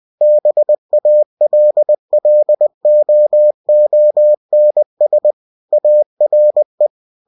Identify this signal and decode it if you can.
Morse code, heard as a keyed tone, about 20 wpm, 600 Hz. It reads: BALLOONS ARE
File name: cw-books / winnie-the-pooh / winnie-20wpm-page-0058